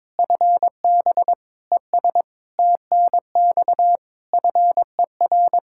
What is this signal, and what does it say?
Morse code, heard as a keyed tone, about 22 wpm, 700 Hz. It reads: FB ES TNX FER
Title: Practice QSO - Exchange 2 - Info - 22wpm